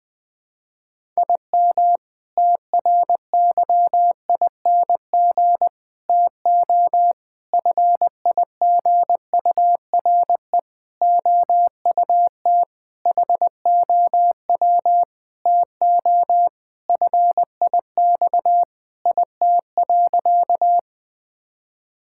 Morse code, heard as a keyed tone, about 21 wpm, 700 Hz. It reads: IM TRYING TO FIGURE OUT HOW TO FIX IT.
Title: Sentences with 6-Word Phrases - Spoken Beginning - 20wpm